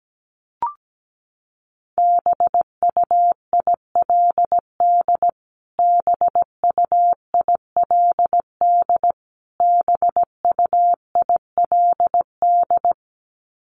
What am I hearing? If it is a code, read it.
Morse code, heard as a keyed tone, about 17 wpm, 700 Hz. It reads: BUILD BUILD BUILD
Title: Top 200 Words Repeated 17wpm